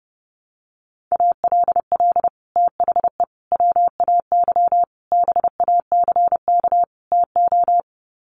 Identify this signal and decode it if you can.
Morse code, heard as a keyed tone, about 30 wpm, 700 Hz. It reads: ALL THE WAY BACK TO
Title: Sets of 5 Words Encore 30wpm